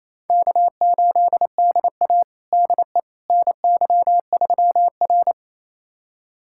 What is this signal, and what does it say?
Morse code, heard as a keyed tone, about 28 wpm, 700 Hz. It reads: K8DA DE NY3R
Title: USA Callsign DE Callsign - 4 Characters - 28wpm